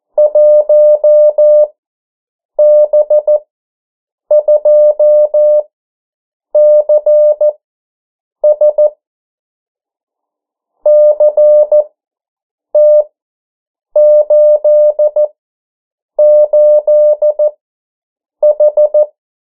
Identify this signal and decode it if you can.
Morse code, heard as a keyed tone, about 14 wpm, spaced as for 7 wpm, 600 Hz. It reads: 1B2CS CT88H
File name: morse mp3 / 09 urok 8